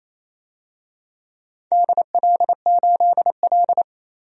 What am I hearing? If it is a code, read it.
Morse code, heard as a keyed tone, about 29 wpm, 700 Hz. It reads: DL8L
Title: DXCC-100 Call Signs - 4 Characters - 28wpm